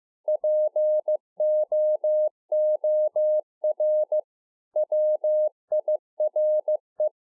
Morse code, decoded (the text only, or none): POOR WIRE